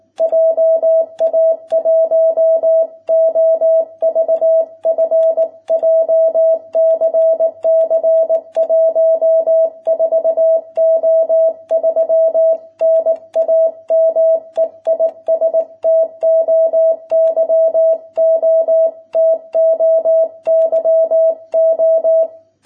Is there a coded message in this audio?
Yes: Morse code from 0.2 to 22.2 s, heard as a keyed tone, about 19 wpm, 660 Hz. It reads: JA1OVF JCC 14O3 NAME IS TOYO TOYO